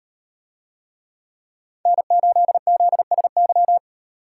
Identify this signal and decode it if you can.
Morse code, heard as a keyed tone, about 38 wpm, 700 Hz. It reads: N8ZSY